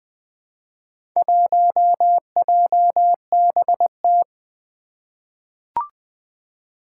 Morse code, heard as a keyed tone, about 20 wpm, 700 Hz. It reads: 1JBT
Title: Four Challenging Letter-Number Combinations 20wpm